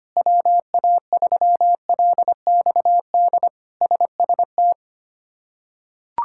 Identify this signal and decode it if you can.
Morse code, heard as a keyed tone, about 25 wpm, 700 Hz. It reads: WA3LXD SST
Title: K1USN SST QSO Practice - 25wpm